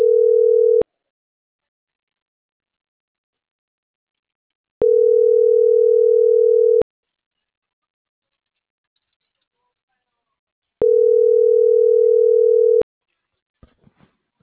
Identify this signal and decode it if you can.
Ringback tone (North America), 440 + 480 Hz.